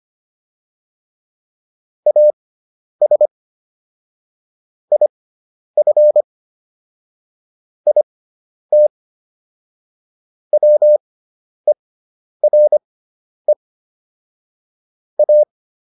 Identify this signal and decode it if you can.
Morse code, heard as a keyed tone, about 25 wpm, spaced as for 10 wpm, 600 Hz. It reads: AS IF IT WERE A